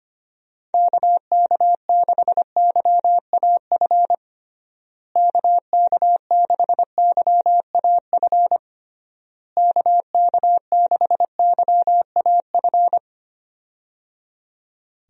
Morse code, heard as a keyed tone, about 25 wpm, 700 Hz. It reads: KK6YAF KK6YAF KK6YAF
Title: USA Call Signs - 6 Characters - Repeated 3x - 25wpm